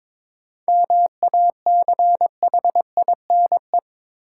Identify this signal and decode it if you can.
Morse code, heard as a keyed tone, about 22 wpm, 700 Hz. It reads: MACHINE
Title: Sentences from Top 1000 Words - Review Part 2 - New Word Drill - 22wpm